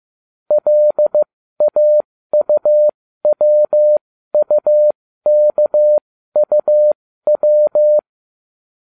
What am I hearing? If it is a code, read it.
Morse code, heard as a keyed tone, about 15 wpm, spaced as for 13 wpm, 600 Hz. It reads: LAUWUKUW